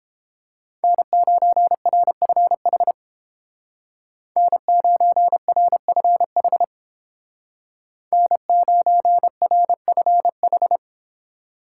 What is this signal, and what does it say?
Morse code, heard as a keyed tone, about 29 wpm, 700 Hz. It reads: N9RFH N9RFH N9RFH